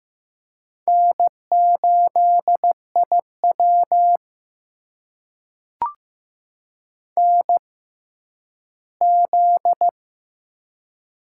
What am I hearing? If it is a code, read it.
Morse code, heard as a keyed tone, about 15 wpm, spaced as for 10 wpm, 700 Hz. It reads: N8IW NZ
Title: USA Call Signs - 4 Characters - 6x Character Spacing - 15wpm